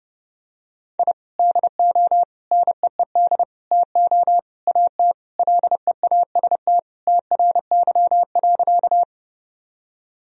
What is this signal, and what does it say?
Morse code, heard as a keyed tone, about 32 wpm, 700 Hz. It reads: I DO NEED TO AT LEAST TRY.
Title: Sentences from Top 1000 Words - Review Part 1 - 30wpm